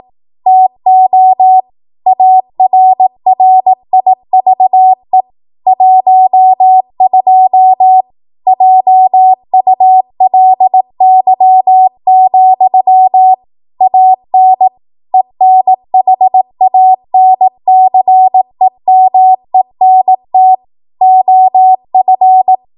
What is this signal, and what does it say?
Morse code, heard as a keyed tone, about 18 wpm, 750 Hz. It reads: TO ARRIVE 12 JULY, AN ENHANCEMENT OF
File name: Morse / 18WPM